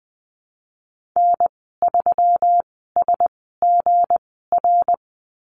Morse code, heard as a keyed tone, about 20 wpm, spaced as for 15 wpm, 700 Hz. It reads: N3SGR
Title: 173 Lesson 28g - Number 9 - Call Signs 20wpm